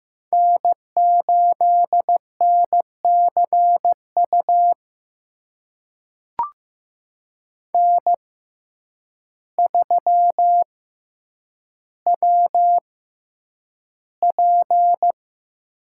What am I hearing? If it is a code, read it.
Morse code, heard as a keyed tone, about 15 wpm, spaced as for 8 wpm, 700 Hz. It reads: N8NCU N3WP
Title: USA Call Signs - 5 Characters - 6x Character Spacing - 15wpm